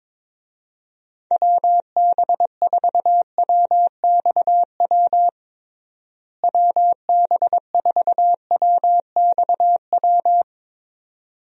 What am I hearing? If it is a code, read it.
Morse code, heard as a keyed tone, about 22 wpm, 700 Hz. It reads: WB4WXW WB4WXW
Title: USA Call Signs - 6 Characters - Repeated 2x - 22wpm